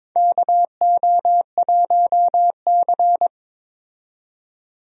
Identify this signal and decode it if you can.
Morse code, heard as a keyed tone, about 22 wpm, 700 Hz. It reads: KO1C
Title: USA Call Signs - 4 Characters - 22wpm